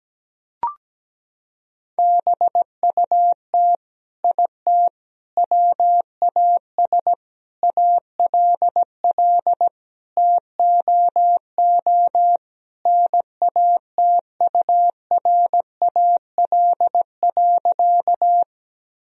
Morse code, heard as a keyed tone, about 17 wpm, 700 Hz. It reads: BUT IT WAS ALL TOO NATURAL.
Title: Sentences from Top 1000 Words - Review Part 4 - 17wpm